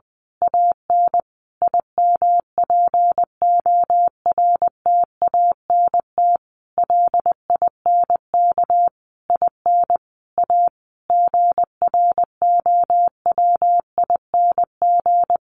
Morse code, heard as a keyed tone, about 20 wpm, 700 Hz. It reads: AN IMPORTANT LINK IN A GROWING
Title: A Sound Of Thunder 20wpm